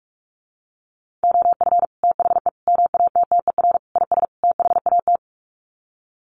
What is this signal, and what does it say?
Morse code, heard as a keyed tone, about 44 wpm, 700 Hz. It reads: OF THE MATTER IS THAT
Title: Sets of 5 Words Encore 45wpm